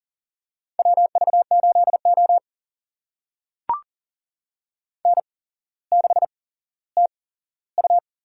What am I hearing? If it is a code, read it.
Morse code, heard as a keyed tone, about 40 wpm, spaced as for 17 wpm, 700 Hz. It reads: WV8O N6TU
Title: USA Call Signs - 4 Characters - 8x Character Spacing - 40wpm